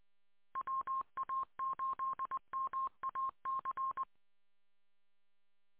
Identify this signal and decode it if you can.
Morse code, heard as a keyed tone, about 20 wpm, 1060 Hz. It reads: WA8MAC